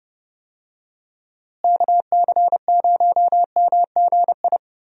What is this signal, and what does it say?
Morse code, heard as a keyed tone, about 31 wpm, 700 Hz. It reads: KC0MGI
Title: USA Call Signs - 6 Characters - Repeated 3x - 30wpm